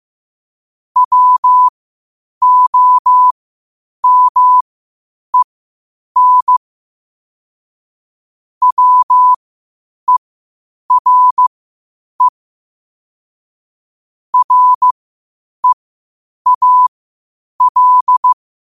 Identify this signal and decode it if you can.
Morse code, heard as a keyed tone, about 15 wpm, spaced as for 8 wpm, 1000 Hz. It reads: WOMEN WERE REAL
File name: qotdfiles / 8wpm